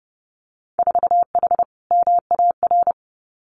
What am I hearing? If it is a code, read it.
Morse code, heard as a keyed tone, about 30 wpm, 700 Hz. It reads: 4H MAR
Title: Winter Field Day Exchanges Encore 30wpm